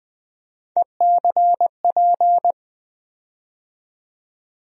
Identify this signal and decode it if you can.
Morse code, heard as a keyed tone, about 20 wpm, 700 Hz. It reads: ECP
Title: USA Call Sign Postfixes - 3 Letters 20wpm